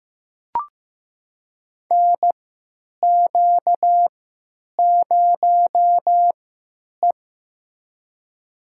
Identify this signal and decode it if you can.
Morse code, heard as a keyed tone, about 15 wpm, spaced as for 9 wpm, 700 Hz. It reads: NQ0E
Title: USA Call Signs - 4 Characters - 3x Character Spacing - 15wpm